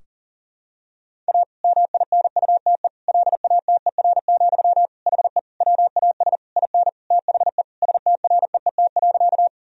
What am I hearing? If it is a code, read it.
Morse code, heard as a keyed tone, about 40 wpm, 700 Hz. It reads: A MINUTE LATER, HE WAS IN THE STREET.
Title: Sentences from Top 1000 Words - Review Part 4 - 40wpm